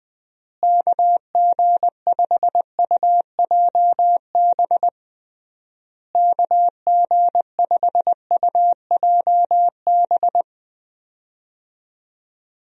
Morse code, heard as a keyed tone, about 21 wpm, 700 Hz. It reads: KG5UJB KG5UJB
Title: USA Call Signs - 6 Characters - Repeated 3x - 20wpm